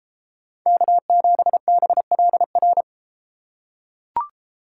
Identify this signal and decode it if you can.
Morse code, heard as a keyed tone, about 31 wpm, 700 Hz. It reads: K7BLR